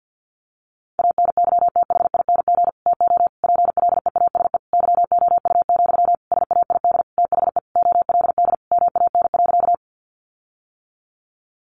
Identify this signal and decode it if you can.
Morse code, heard as a keyed tone, about 50 wpm, 700 Hz. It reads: ANYTHING TO PLEASE YOU, SAID THE OLD MAN.
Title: Sentences from Top 1500 Words 50wpm